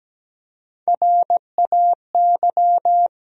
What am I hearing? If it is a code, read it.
Morse code, heard as a keyed tone, about 17 wpm, 700 Hz. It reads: RAY
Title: USA Common Names - Max 4 Letters - 17wpm